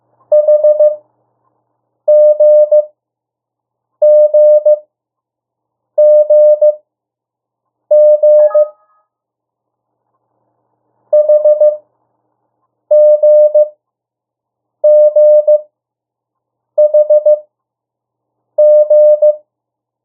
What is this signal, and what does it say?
Morse code, heard as a keyed tone, about 15 wpm, spaced as for 6 wpm, 600 Hz. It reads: HGGGG HGGHG
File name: morse mp3 / 02 urok 1